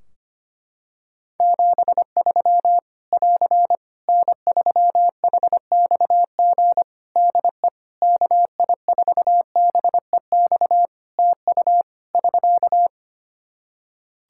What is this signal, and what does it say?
Morse code, heard as a keyed tone, about 25 wpm, 700 Hz. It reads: 73 <AR> N3HXG DE KI4BEX TU <SK>
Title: K1USN SST QSO Practice 25wpm